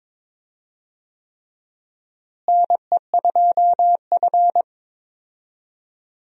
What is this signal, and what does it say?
Morse code, heard as a keyed tone, about 22 wpm, 700 Hz. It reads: NE2F
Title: USA Call Signs - 4 Characters - Repeated 3x - 22wpm